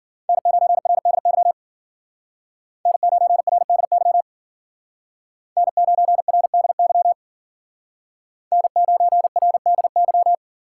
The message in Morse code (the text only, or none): N9RDY N9RDY N9RDY N9RDY